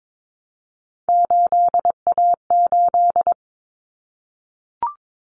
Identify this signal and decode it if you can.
Morse code, heard as a keyed tone, about 22 wpm, 700 Hz. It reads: 8A8
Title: Common Cut Numbers - 3-Digit 22wpm